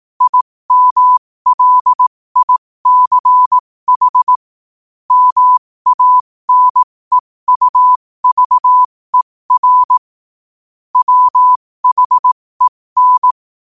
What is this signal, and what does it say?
Morse code, heard as a keyed tone, about 18 wpm, spaced as for 15 wpm, 1000 Hz. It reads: IMLICH MANEUVER WHEN
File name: qotdfiles / 15wpm